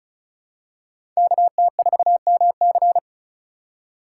Morse code, heard as a keyed tone, about 34 wpm, 700 Hz. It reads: KT4MC
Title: USA Call Signs - 5 Characters - 3x Character Spacing - 35wpm